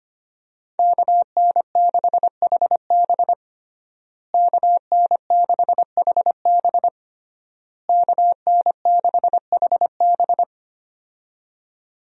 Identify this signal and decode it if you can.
Morse code, heard as a keyed tone, about 25 wpm, 700 Hz. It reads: KN6HB KN6HB KN6HB
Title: USA Call Signs - 5 Characters - Repeated 3x - 25wpm